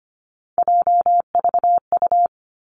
Morse code, heard as a keyed tone, about 25 wpm, 700 Hz. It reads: JVU
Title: Three Challenging Letter-Number Combinations 25wpm